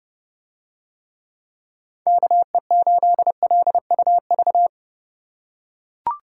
Morse code, heard as a keyed tone, about 27 wpm, 700 Hz. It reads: KE8LUV